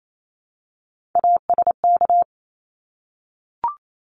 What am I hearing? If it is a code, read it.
Morse code, heard as a keyed tone, about 28 wpm, 700 Hz. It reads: ASK
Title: Top 300 Words Repeated 28wpm